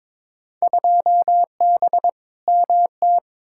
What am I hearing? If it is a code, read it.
Morse code, heard as a keyed tone, about 22 wpm, 700 Hz. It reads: 2B MT
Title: Field Day Exchanges Encore - 22wpm